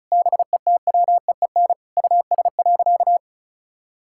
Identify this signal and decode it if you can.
Morse code, heard as a keyed tone, about 36 wpm, 700 Hz. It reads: BETWEEN US.
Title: Sentences from Top 1000 Words - Review Part 6 - 35wpm